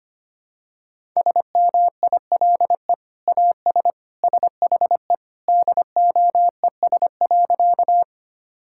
Morse code, heard as a keyed tone, about 25 wpm, 700 Hz. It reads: SMILE AS SHE DOES.
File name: practice / 25wpm / Sentences with 2-Word Phrases - 25wpm